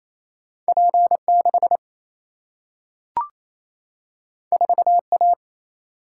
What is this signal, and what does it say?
Morse code, heard as a keyed tone, about 28 wpm, 700 Hz. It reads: P6 4A
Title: Single-Letter Single-Number 28wpm